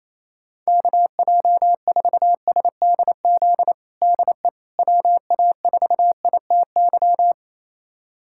Morse code, heard as a keyed tone, about 28 wpm, 700 Hz. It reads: KJ4SDZ DE WA4ITY
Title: USA Callsign DE Callsign - 6 Characters - 28wpm